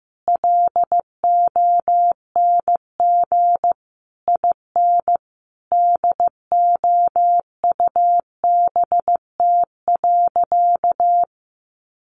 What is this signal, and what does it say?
Morse code, heard as a keyed tone, about 14 wpm, 700 Hz. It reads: LONG IN DOUBT.
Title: Sentences from Top 1500 Words 15wpm